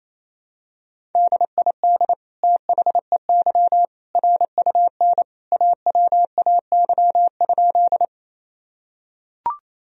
Morse code, heard as a keyed tone, about 28 wpm, 700 Hz. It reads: DID THEY RUN AWAY?
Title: Sentences from Top 1000 Words - Review Part 1 - 28wpm